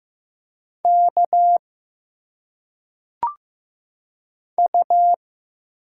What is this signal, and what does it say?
Morse code, heard as a keyed tone, about 15 wpm, 700 Hz. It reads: K U
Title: Single Letters - Phonetic Pronunciation 15wpm